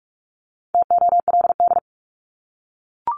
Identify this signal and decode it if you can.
Morse code, heard as a keyed tone, about 44 wpm, 700 Hz. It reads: TOLD
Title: Top 500 Words Repeated 45wpm